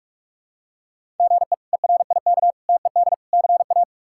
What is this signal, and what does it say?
Morse code, heard as a keyed tone, about 46 wpm, 700 Hz. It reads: GE ERIK TED CA
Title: K1USN SST QSO Practice - 45wpm